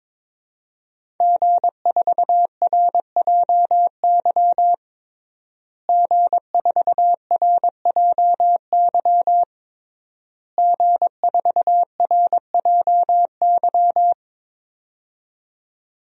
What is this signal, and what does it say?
Morse code, heard as a keyed tone, about 22 wpm, 700 Hz. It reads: G4RJY G4RJY G4RJY
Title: DXCC-100 Call Signs - 5 Characters - Repeated 3x - 22wpm